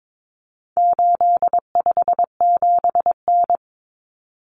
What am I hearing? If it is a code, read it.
Morse code, heard as a keyed tone, about 22 wpm, 700 Hz. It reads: 857N
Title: Common Cut Numbers - 4-Digit 22wpm